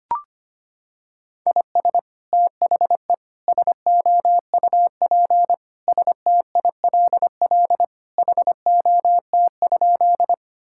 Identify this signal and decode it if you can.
Morse code, heard as a keyed tone, about 25 wpm, 700 Hz. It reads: IS THE SOUP STILL HOT?
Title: Sentences from Top 1000 Words - Review Part 6 - 25wpm